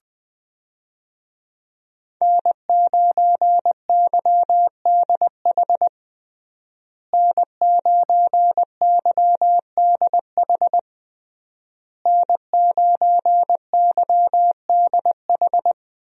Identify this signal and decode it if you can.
Morse code, heard as a keyed tone, about 20 wpm, 700 Hz. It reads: N9YDH N9YDH N9YDH
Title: USA Call Signs - 5 Characters - Repeated 3x - 20wpm